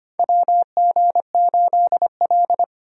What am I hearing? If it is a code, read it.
Morse code, heard as a keyed tone, about 25 wpm, 700 Hz. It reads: WG8L
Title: USA Call Signs - 4 Characters - Repeated 3x - 25wpm